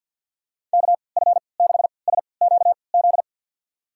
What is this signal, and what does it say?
Morse code, heard as a keyed tone, about 47 wpm, spaced as for 28 wpm, 700 Hz. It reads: KF6SQZ
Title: USA Call Signs - 3x Character Spacing - 50wpm